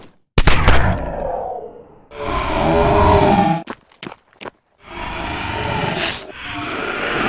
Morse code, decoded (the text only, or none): none